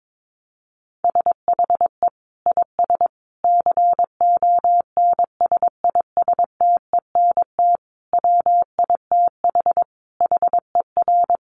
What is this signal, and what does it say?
Morse code, heard as a keyed tone, about 22 wpm, 700 Hz. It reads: SHE IS CONSISTENT WITH HER
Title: Sentences from Top 1500 Words 22wpm